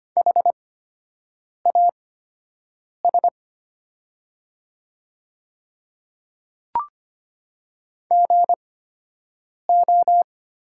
Morse code, heard as a keyed tone, about 25 wpm, spaced as for 7 wpm, 700 Hz. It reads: HAS GO
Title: 8x Character Spacing - Top 100 Words - 25wpm